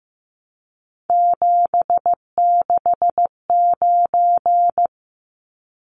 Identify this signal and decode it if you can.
Morse code, heard as a keyed tone, about 15 wpm, 700 Hz. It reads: 769
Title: Numbers - 3-Digit 15wpm